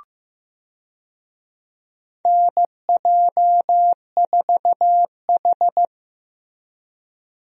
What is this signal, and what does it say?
Morse code, heard as a keyed tone, about 15 wpm, 700 Hz. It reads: NJ4H
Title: USA Call Signs - 4 Characters - Repeated 2x - 15wpm